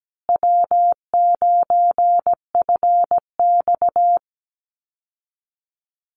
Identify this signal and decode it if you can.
Morse code, heard as a keyed tone, about 17 wpm, 700 Hz. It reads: W9FX
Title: Call Signs Repeated 3x - 17wpm